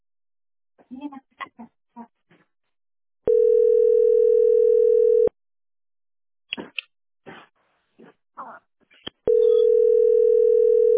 Ringback tone (North America), 440 + 480 Hz.